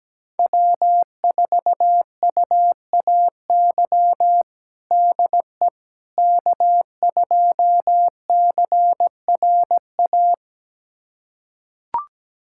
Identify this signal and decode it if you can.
Morse code, heard as a keyed tone, about 17 wpm, 700 Hz. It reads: W4UAY DE K2CRA
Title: USA Callsign DE Callsign - 5 Characters - 17wpm